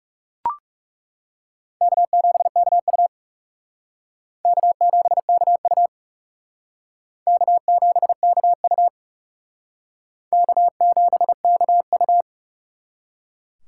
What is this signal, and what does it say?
Morse code, heard as a keyed tone, about 39 wpm, 700 Hz. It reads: K7KU K7KU K7KU K7KU